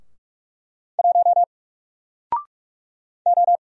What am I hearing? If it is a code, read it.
Morse code, heard as a keyed tone, about 44 wpm, 700 Hz. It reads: 1 O
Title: Single Letter-Number Alt 45wpm